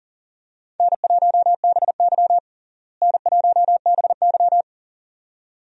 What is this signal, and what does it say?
Morse code, heard as a keyed tone, about 40 wpm, 700 Hz. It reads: N1BY N1BY
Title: USA Call Signs - 4 Characters - Repeated 2x - 40wpm